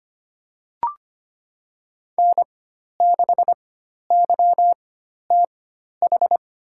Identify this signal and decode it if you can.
Morse code, heard as a keyed tone, about 25 wpm, spaced as for 12 wpm, 700 Hz. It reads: N6YTH
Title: USA Call Signs - 4x Character Spacing - 25wpm